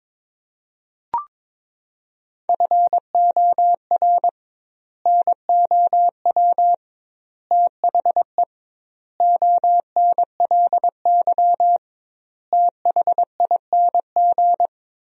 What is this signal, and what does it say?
Morse code, heard as a keyed tone, about 22 wpm, 700 Hz. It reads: FOR NOW THE ONLY THING
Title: 2x Word-Spacing - Sentences from Top 200 Words - 22wpm